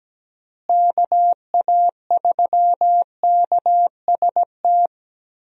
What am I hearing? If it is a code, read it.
Morse code, heard as a keyed tone, about 17 wpm, 700 Hz. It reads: KA3KST
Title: USA Call Signs - 6 Characters - Repeated 3x - 17wpm